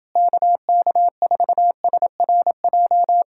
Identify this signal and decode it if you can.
Morse code, heard as a keyed tone, about 28 wpm, 700 Hz. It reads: KK4SRJ